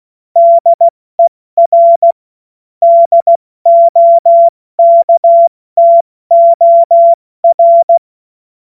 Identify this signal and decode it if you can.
Morse code, heard as a keyed tone, about 16 wpm, 680 Hz. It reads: DER DOKTOR
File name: Morsekurs / Witze / Witz 04. Juli 2023 70 F80